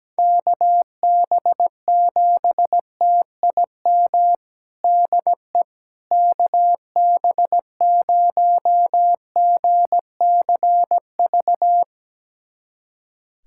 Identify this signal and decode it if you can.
Morse code, heard as a keyed tone, about 17 wpm, 700 Hz. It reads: KB7TIM DE KB0GCV
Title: USA Callsign DE Callsign - 6 Characters - 17wpm